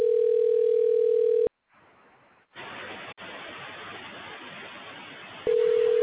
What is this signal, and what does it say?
Ringback tone (North America), 440 + 480 Hz.